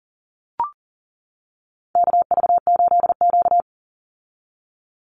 Morse code, heard as a keyed tone, about 38 wpm, 700 Hz. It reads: KV8Q
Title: Call Signs - 40wpm